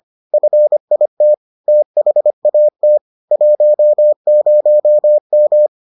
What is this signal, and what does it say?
Morse code, heard as a keyed tone, about 25 wpm, 600 Hz. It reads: FIT THAT 10M